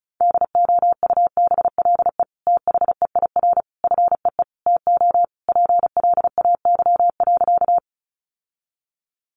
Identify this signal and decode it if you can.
Morse code, heard as a keyed tone, about 34 wpm, 700 Hz. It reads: DOUBLE THEIR FEE TO PLAY.
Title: Sentences from Top 1500 Words 35wpm